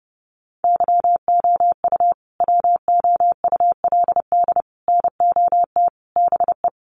Morse code, heard as a keyed tone, about 30 wpm, 700 Hz. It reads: YOU WOULD NOT BE
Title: Sets of 5 Words Encore2 30wpm